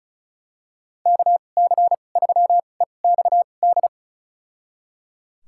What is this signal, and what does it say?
Morse code, heard as a keyed tone, about 34 wpm, spaced as for 25 wpm, 700 Hz. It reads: KC3EXD
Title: USA Call Signs - 6 Characters - 2x Character Spacing - 35wpm